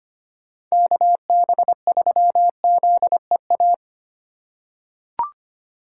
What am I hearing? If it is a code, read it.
Morse code, heard as a keyed tone, about 25 wpm, 700 Hz. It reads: KB3ZEA
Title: USA Call Signs - 6 Characters - 2x Character Spacing - 25wpm